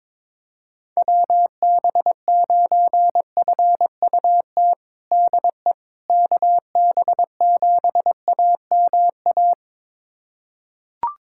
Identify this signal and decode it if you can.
Morse code, heard as a keyed tone, about 22 wpm, 700 Hz. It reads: WB9FUT DE KB7AMA
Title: USA Callsign DE Callsign - 6 Characters - 22wpm